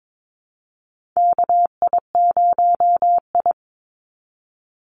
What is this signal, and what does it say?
Morse code, heard as a keyed tone, about 22 wpm, 700 Hz. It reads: KI0I
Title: Call Signs Repeated 2x - 22wpm